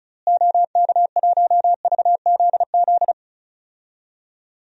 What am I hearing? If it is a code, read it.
Morse code, heard as a keyed tone, about 34 wpm, 700 Hz. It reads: OK1VZZ
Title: DXCC-100 Call Signs - 6 Characters - Repeated 2x - 35wpm